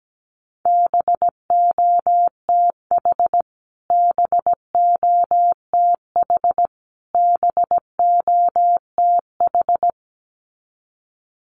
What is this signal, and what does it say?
Morse code, heard as a keyed tone, about 17 wpm, 700 Hz. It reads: BOTH BOTH BOTH
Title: Top 500 Words Repeated 17wpm